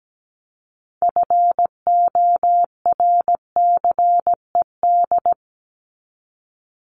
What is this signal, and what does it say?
Morse code, heard as a keyed tone, about 17 wpm, 700 Hz. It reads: FORCED
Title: ED Suffix - Top 100 Words with Base Words 17wpm